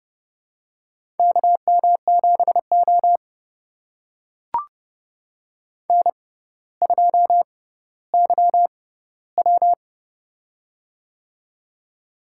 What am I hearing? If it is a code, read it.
Morse code, heard as a keyed tone, about 30 wpm, spaced as for 15 wpm, 700 Hz. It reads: KM7O N2YW
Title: USA Call Signs - 4 Characters - 6x Character Spacing - 30wpm